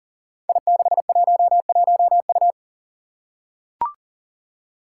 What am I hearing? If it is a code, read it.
Morse code, heard as a keyed tone, about 40 wpm, 700 Hz. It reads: I611U
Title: Five Challenging Letter-Number Combinations 40wpm